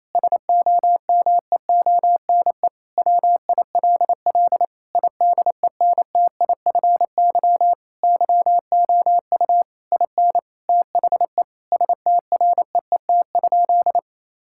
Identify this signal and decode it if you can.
Morse code, heard as a keyed tone, about 28 wpm, 690 Hz. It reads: SOMEONE WILL IDENTIFY YOU IN THE STREET?
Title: Sentences from Top 1000 Words - Review Part 6 - 28wpm